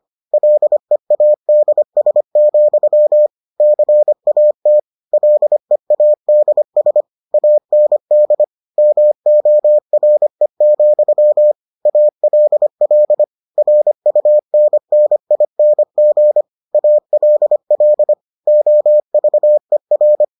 Morse code, heard as a keyed tone, about 25 wpm, 600 Hz. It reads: LEADS, CAT LEADS AND MORE, ALL RUNNING ALL OVER